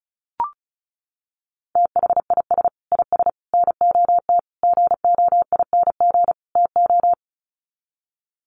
Morse code, heard as a keyed tone, about 34 wpm, 700 Hz. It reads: THIS IS NOT GOING TO
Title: Sets of 5 Words Encore2 35wpm